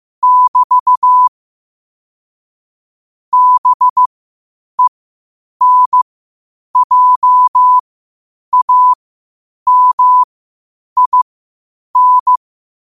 Morse code, heard as a keyed tone, about 15 wpm, spaced as for 8 wpm, 1000 Hz. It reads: <BT> BENJAMIN